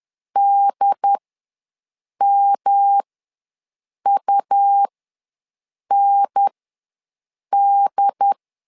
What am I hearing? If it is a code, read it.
Morse code, heard as a keyed tone, about 10 wpm, spaced as for 6 wpm, 800 Hz. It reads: DMUND
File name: LEZIONI / 3 MWRO / 3 M ws 10wpm 2121